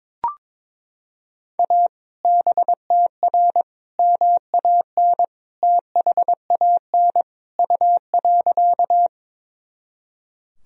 Morse code, heard as a keyed tone, about 22 wpm, 700 Hz. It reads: A BTR MAN THAN U.
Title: Sentences from Top 400 Words and Abbreviations 22wpm